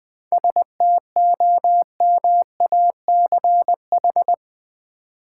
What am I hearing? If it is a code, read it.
Morse code, heard as a keyed tone, about 20 wpm, 700 Hz. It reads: STOMACH
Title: 7 Letter Words - First 3 Letters Given - 20wpm